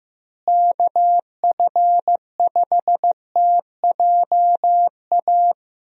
Morse code, heard as a keyed tone, about 15 wpm, 700 Hz. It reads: KF5TJA
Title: USA Call Signs - 6 Characters - Repeated 2x - 15wpm